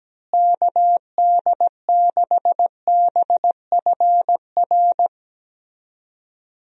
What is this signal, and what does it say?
Morse code, heard as a keyed tone, about 17 wpm, 700 Hz. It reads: KD6BFR
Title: USA Call Signs - 6 Characters - Repeated 2x - 17wpm